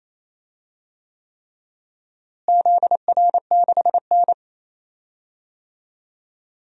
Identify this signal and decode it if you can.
Morse code, heard as a keyed tone, about 28 wpm, 700 Hz. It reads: ZR6N
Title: DXCC-100 Call Signs - 4 Characters - Repeated 3x - 28wpm